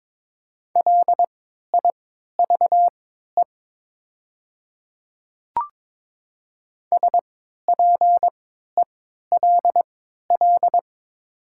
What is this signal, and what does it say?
Morse code, heard as a keyed tone, about 23 wpm, spaced as for 13 wpm, 700 Hz. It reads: LIVE SPELL
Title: 3x Character Spacing - Top 200 Words - 22wpm